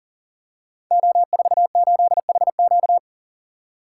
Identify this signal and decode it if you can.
Morse code, heard as a keyed tone, about 42 wpm, 700 Hz. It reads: O48HQ
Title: Code Groups - 5 Letter:Numbers - 40wpm